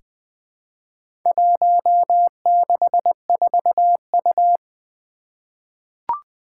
Morse code, heard as a keyed tone, about 20 wpm, 700 Hz. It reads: 164U
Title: Four Challenging Letter-Number Combinations 20wpm